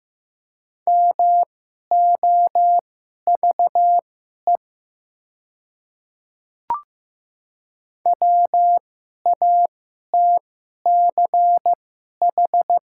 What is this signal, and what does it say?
Morse code, heard as a keyed tone, about 15 wpm, spaced as for 11 wpm, 700 Hz. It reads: MOVE WATCH
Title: 2x Character Spacing - Top 300 Words - 15wpm